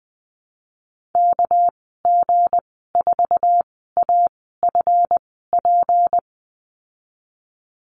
Morse code, heard as a keyed tone, about 20 wpm, 700 Hz. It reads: K G 4 A F P
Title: 238 Lesson 38g - Letter q - Call Signs 20wpm